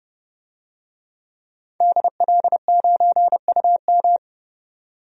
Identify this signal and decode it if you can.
Morse code, heard as a keyed tone, about 32 wpm, 700 Hz. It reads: DL9UM
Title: DXCC-100 Call Signs - 5 Characters - 30wpm